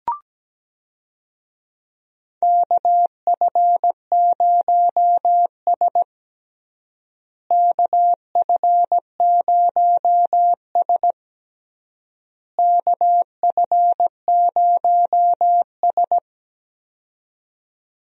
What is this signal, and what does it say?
Morse code, heard as a keyed tone, about 17 wpm, 700 Hz. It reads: KF0S KF0S KF0S
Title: USA Call Signs - 4 Characters - Repeated 3x - 17wpm